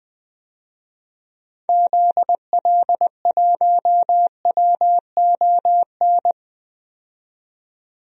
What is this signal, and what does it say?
Morse code, heard as a keyed tone, about 21 wpm, 700 Hz. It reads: ZL1WON
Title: DXCC-100 Call Signs - 6 Characters - 20wpm